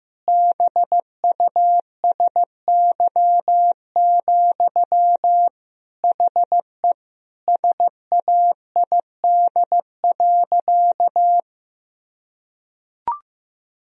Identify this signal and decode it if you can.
Morse code, heard as a keyed tone, about 15 wpm, 700 Hz. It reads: BUSY, HE SAID.
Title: Sentences from Top 1000 Words - Review Part 3 - 15wpm